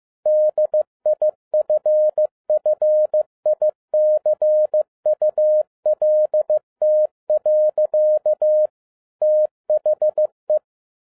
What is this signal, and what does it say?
Morse code, heard as a keyed tone, about 15 wpm, 610 Hz. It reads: DIFFICULT. THE